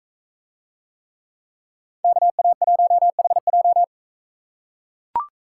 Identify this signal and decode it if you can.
Morse code, heard as a keyed tone, about 42 wpm, 700 Hz. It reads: KA1HJ